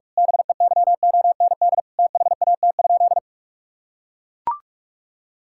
Morse code, heard as a keyed tone, about 41 wpm, 700 Hz. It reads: BEYOND THAT?
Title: Sentences from Top 1000 Words - Review Part 6 - 45wpm